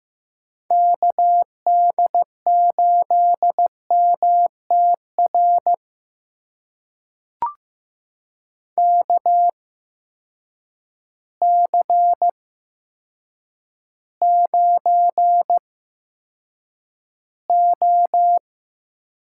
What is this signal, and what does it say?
Morse code, heard as a keyed tone, about 15 wpm, spaced as for 8 wpm, 700 Hz. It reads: KD8MTR KC9O
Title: USA Call Signs - 6 Characters - 8x Character Spacing - 15wpm